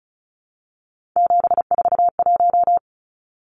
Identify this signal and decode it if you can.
Morse code, heard as a keyed tone, about 34 wpm, 700 Hz. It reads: 741
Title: Numbers - 3-Digit 35wpm